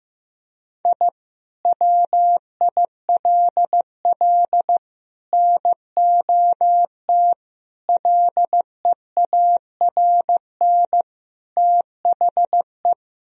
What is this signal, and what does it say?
Morse code, heard as a keyed tone, about 16 wpm, 700 Hz. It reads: I WILL NOT LEARN THE
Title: Sentences from Top 1000 Words - Review Part 3 - 15wpm